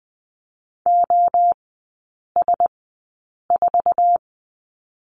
Morse code, heard as a keyed tone, about 20 wpm, spaced as for 8 wpm, 700 Hz. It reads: OS4
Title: 047 Lesson 9e - Number 4 - Three Characters 20wpm